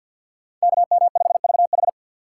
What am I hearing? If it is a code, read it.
Morse code, heard as a keyed tone, about 52 wpm, 700 Hz. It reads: KM5VH
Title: USA Call Signs - 5 Characters - 2x Character Spacing - 50wpm